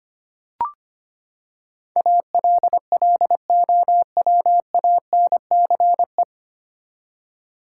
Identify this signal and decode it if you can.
Morse code, heard as a keyed tone, about 25 wpm, 700 Hz. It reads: ALLOWANCE
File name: Nine Letter Words - 25wpm